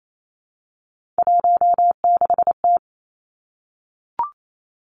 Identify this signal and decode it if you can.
Morse code, heard as a keyed tone, about 28 wpm, 700 Hz. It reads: 16T
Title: Three Challenging Letter-Number Combinations 28wpm